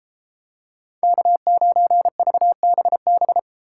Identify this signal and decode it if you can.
Morse code, heard as a keyed tone, about 31 wpm, 700 Hz. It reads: K9VBB